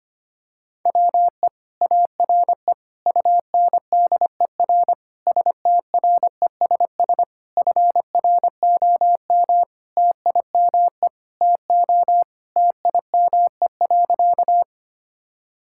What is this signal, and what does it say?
Morse code, heard as a keyed tone, about 25 wpm, 700 Hz. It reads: WE ARE UNDER STRESS FROM TIME TO TIME.
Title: Sentences from Top 1000 Words - Review Part 6 - 25wpm